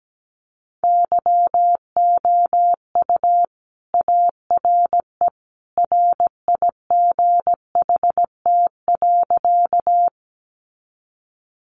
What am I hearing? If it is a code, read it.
Morse code, heard as a keyed tone, about 17 wpm, 700 Hz. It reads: YOU ARE RIGHT.
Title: Sentences from Top 200 Words 17wpm